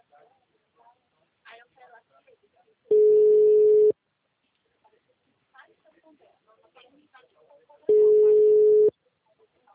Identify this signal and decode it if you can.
Ringback tone (Europe, 425 Hz), with speech.